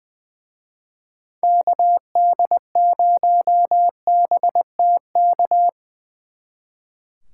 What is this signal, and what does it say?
Morse code, heard as a keyed tone, about 23 wpm, 700 Hz. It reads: KD0BTK
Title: Speed Racing - USA 2x3 Call Signs 20wpm